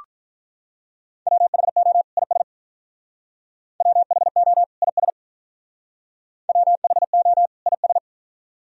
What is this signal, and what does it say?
Morse code, heard as a keyed tone, about 46 wpm, 700 Hz. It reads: WHO IS WHO IS WHO IS